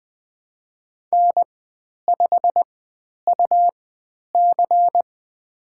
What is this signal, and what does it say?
Morse code, heard as a keyed tone, about 20 wpm, spaced as for 10 wpm, 700 Hz. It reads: N5UC
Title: 102 Lesson 17g - Letter u - Call Signs 20wpm